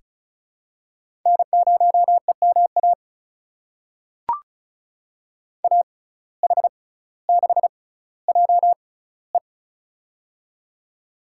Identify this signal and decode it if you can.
Morse code, heard as a keyed tone, about 36 wpm, spaced as for 18 wpm, 700 Hz. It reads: N0EMA AH6JE
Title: USA Call Signs - 5 Characters - 6x Character Spacing - 35wpm